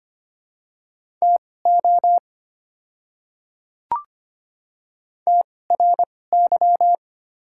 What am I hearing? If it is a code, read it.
Morse code, heard as a keyed tone, about 25 wpm, spaced as for 18 wpm, 700 Hz. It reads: TO TRY
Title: 2x Character Spacing - Top 200 Words - 25wpm